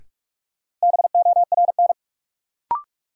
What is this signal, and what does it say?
Morse code, heard as a keyed tone, about 47 wpm, 700 Hz. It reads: BORN
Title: Sentences from Top 1000 Words - New Word Drill - 45wpm